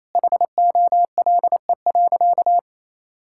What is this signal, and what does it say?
Morse code, heard as a keyed tone, about 28 wpm, 700 Hz. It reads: HOLE.
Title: Sentences from Top 1600 Words 28wpm